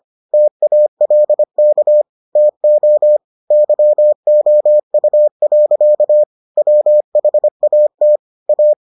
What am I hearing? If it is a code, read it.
Morse code, heard as a keyed tone, about 25 wpm, 600 Hz. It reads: TALK TO YOU. WHAT A